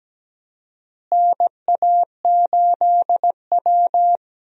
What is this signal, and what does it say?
Morse code, heard as a keyed tone, about 17 wpm, 700 Hz. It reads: NA8W
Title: USA Call Signs - 4 Characters - Repeated 3x - 17wpm